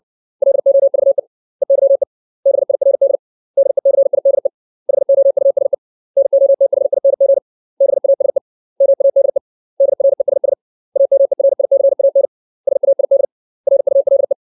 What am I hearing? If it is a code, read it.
Morse code, heard as a keyed tone, about 60 wpm, 550 Hz. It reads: LOVE EYE BIRD DECIDE HOUSE NOTHING BASE MADE DRESS AMERICAN SAID RULE
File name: 632 words / 632 Words 60WPM